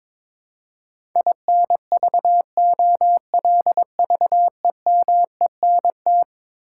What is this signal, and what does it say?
Morse code, heard as a keyed tone, about 22 wpm, 700 Hz. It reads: INVOLVEMENT
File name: practice / 22wpm / Eleven Letter Words - 22wpm